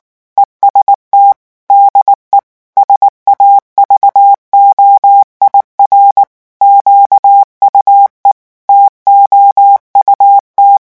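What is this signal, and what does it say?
Morse code, heard as a keyed tone, about 19 wpm, 800 Hz. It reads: EST DE SAVOIR QUE TOUT
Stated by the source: French Morse revision 8 drill of random five-letter groups at 18 words per minute.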